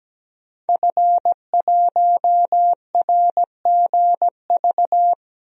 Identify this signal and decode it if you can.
Morse code, heard as a keyed tone, about 17 wpm, 700 Hz. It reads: F1RGV
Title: DXCC-100 Call Signs - 5 Characters - 17wpm